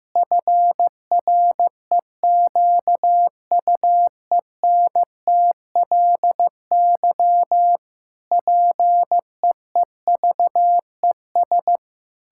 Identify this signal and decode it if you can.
Morse code, heard as a keyed tone, about 14 wpm, 700 Hz. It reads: FREQUENTLY PEEVES
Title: Pangram Challenge v2 - 15wpm